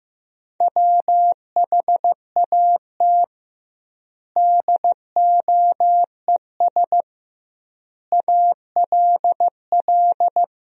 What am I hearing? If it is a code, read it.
Morse code, heard as a keyed tone, about 15 wpm, 700 Hz. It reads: WHAT DOES ALL
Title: 2x Word-Spacing - Sentences from Top 200 Words - 15wpm